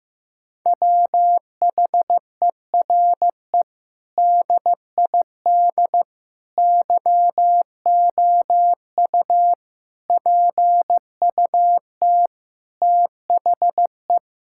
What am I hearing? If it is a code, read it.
Morse code, heard as a keyed tone, about 15 wpm, 700 Hz. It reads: WHERE DID YOU PUT THE
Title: Sentences from Top 1600 Words 15wpm